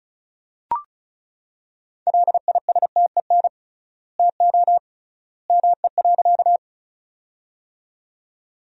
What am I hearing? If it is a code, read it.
Morse code, heard as a keyed tone, about 36 wpm, 700 Hz. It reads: LISTEN TO ME.
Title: 3x Word-Spacing - Sentences from Top 400 Words - 35wpm